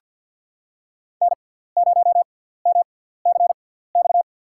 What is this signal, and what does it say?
Morse code, heard as a keyed tone, about 49 wpm, spaced as for 17 wpm, 700 Hz. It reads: N0MCX
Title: USA Call Signs - 5 Characters - 6x Character Spacing - 50wpm